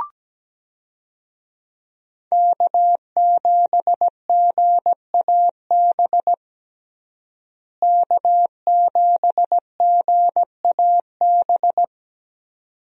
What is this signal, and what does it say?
Morse code, heard as a keyed tone, about 17 wpm, 700 Hz. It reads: K7GAB K7GAB
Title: USA Call Signs - 5 Characters - Repeated 2x - 17wpm